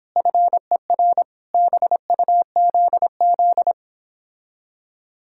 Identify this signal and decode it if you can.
Morse code, heard as a keyed tone, about 29 wpm, 700 Hz. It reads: FER BUZZ